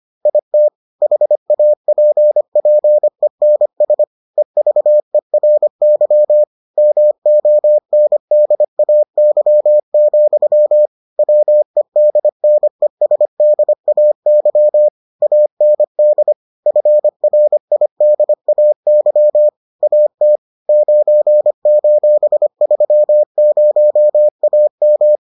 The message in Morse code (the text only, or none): IT HAPPENS EVERY MONDAY, WEDNESDAY AND FRIDAY AT 9:30AM